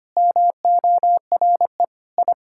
Morse code, heard as a keyed tone, about 25 wpm, 700 Hz. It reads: MORE I
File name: Sets of 9 Words - 25wpm